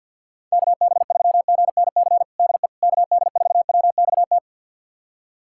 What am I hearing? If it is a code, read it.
Morse code, heard as a keyed tone, about 52 wpm, 700 Hz. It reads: KB3GNC DE KD4WXT
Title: USA Callsign DE Callsign - 6 Characters - 50wpm